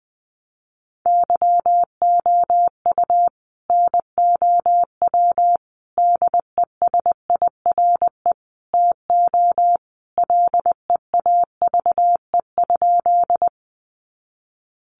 Morse code, heard as a keyed tone, about 20 wpm, 700 Hz. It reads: YOU NOW DESIRE TO LEAVE?
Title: Sentences from Top 1100 Words 20wpm